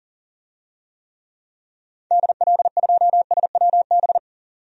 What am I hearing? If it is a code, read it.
Morse code, heard as a keyed tone, about 40 wpm, 700 Hz. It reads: DL2SWB
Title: DXCC-100 Call Signs - 6 Characters - 40wpm